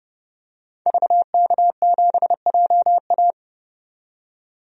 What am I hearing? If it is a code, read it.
Morse code, heard as a keyed tone, about 32 wpm, 700 Hz. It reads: VK7JA
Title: DXCC-100 Call Signs - 5 Characters - 30wpm